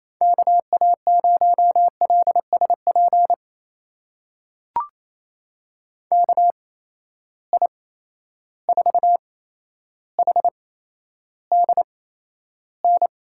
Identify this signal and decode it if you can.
Morse code, heard as a keyed tone, about 28 wpm, spaced as for 12 wpm, 700 Hz. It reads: KA0LSP KI4HDN
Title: USA Call Signs - 6 Characters - 8x Character Spacing - 28wpm